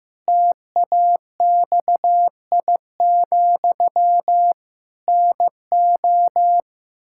Morse code, heard as a keyed tone, about 15 wpm, 700 Hz. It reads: TAXI, NO
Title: Pangram Challenge v2 - 15wpm